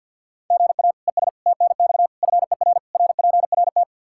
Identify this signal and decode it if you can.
Morse code, heard as a keyed tone, about 49 wpm, 700 Hz. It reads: GA ES TNX FER RPRT
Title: Practice QSO - Exchange 1 - Beginning - 50wpm